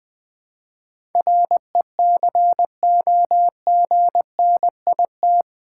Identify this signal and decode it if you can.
Morse code, heard as a keyed tone, about 20 wpm, 700 Hz. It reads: RECOGNIT
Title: Build A Word - 4 Letters At A Time - 11-Letter Words - 20wpm